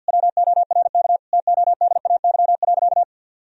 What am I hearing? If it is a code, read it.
Morse code, heard as a keyed tone, about 52 wpm, 700 Hz. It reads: WORK TODAY.
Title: Sentences from Top 1000 Words - Review Part 4 - 50wpm